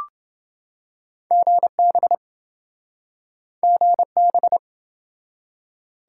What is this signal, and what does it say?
Morse code, heard as a keyed tone, about 28 wpm, 700 Hz. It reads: GB GB